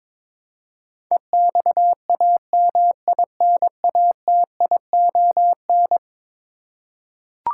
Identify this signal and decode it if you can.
Morse code, heard as a keyed tone, about 22 wpm, 700 Hz. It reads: EXAMINATION
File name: practice / 22wpm / Eleven Letter Words - 22wpm